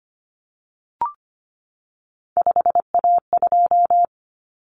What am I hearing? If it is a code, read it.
Morse code, heard as a keyed tone, about 25 wpm, 700 Hz. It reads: 5A2
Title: Common Cut Numbers - 3-Digit 25wpm